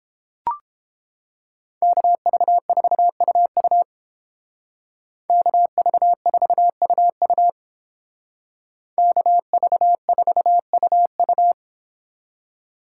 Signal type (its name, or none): Morse code